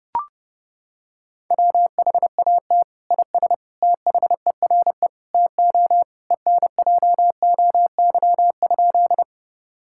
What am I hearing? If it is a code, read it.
Morse code, heard as a keyed tone, about 30 wpm, 700 Hz. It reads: WHAT IS THERE TO ENJOY?
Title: Sentences from Top 1000 Words - Review Part 5 - 30wpm